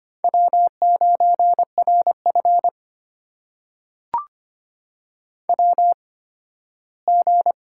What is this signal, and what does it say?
Morse code, heard as a keyed tone, about 25 wpm, spaced as for 15 wpm, 700 Hz. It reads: W9RF WG
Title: USA Call Signs - 4 Characters - 8x Character Spacing - 25wpm